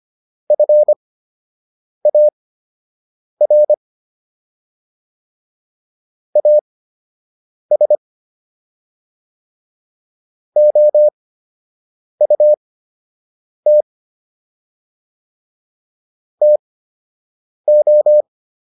Morse code, heard as a keyed tone, about 25 wpm, spaced as for 7 wpm, 600 Hz. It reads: FAR AS OUT TO